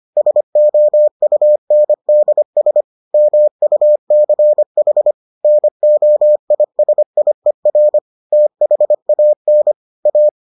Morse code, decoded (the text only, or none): SOUNDS MUCH NOISIER THAN A